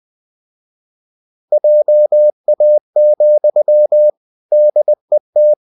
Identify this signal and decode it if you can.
Morse code, heard as a keyed tone, about 20 wpm, 600 Hz. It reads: JA, DET
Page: Ferie 20 wpm